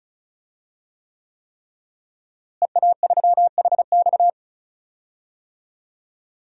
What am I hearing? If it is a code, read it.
Morse code, heard as a keyed tone, about 34 wpm, 700 Hz. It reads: EA3HX
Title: DXCC-100 Call Signs - 5 Characters - Repeated 3x - 35wpm